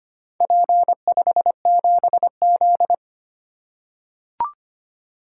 Morse code, heard as a keyed tone, about 25 wpm, 700 Hz. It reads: P57Z
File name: practice / 25wpm / Four Letter-Number Combinations - 25wpm